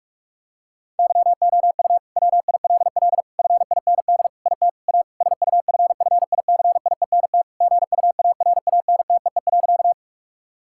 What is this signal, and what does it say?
Morse code, heard as a keyed tone, about 46 wpm, 700 Hz. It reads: YOU WILL FIND IT A SUFFICIENT GUARANTEE.
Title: Sentences from Top 1600 Words 45wpm